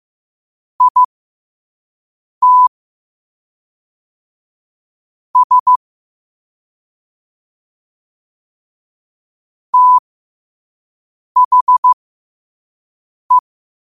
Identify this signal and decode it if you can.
Morse code, heard as a keyed tone, about 15 wpm, spaced as for 5 wpm, 1000 Hz. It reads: IT S THE
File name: qotdfiles / 5wpm